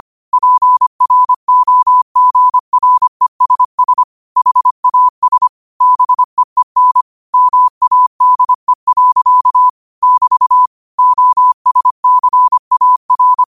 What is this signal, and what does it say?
Morse code, heard as a keyed tone, about 25 wpm, 1000 Hz. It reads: PROGRESS HAS BEEN MADE. <BT> OSCAR